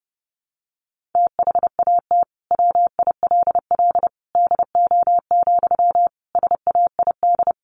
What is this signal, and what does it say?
Morse code, heard as a keyed tone, about 30 wpm, 700 Hz. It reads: THAT WILL DO, SAID
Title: Sentences from Top 100 Words 30wpm